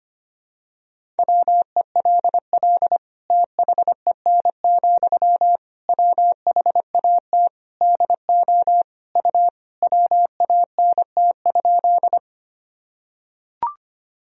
Morse code, heard as a keyed tone, about 25 wpm, 700 Hz. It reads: WELL THEN, WHAT DO U WANT?
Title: Sentences from Top 200 Words and Abbreviations 25wpm